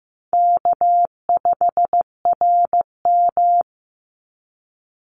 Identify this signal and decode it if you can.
Morse code, heard as a keyed tone, about 15 wpm, 700 Hz. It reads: K5RM
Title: Easy Call Signs Repeated 3x - 15wpm